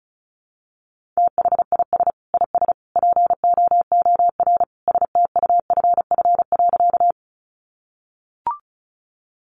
Morse code, heard as a keyed tone, about 34 wpm, 700 Hz. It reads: THIS IS POOR STUFF.
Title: Sentences from Top 1400 Words 35wpm